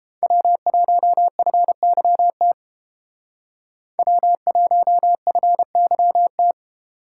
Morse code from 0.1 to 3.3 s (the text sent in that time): W1FYT